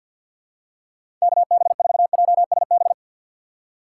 Morse code, heard as a keyed tone, about 52 wpm, 700 Hz. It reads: KB4JSB
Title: USA Call Signs - 6 Characters - 4x Character Spacing - 50wpm